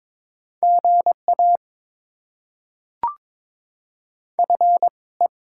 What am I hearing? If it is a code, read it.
Morse code, heard as a keyed tone, about 22 wpm, 690 Hz. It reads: GA F E
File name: Top 20 QSO Elements - 2x Character Spacing - 22wpm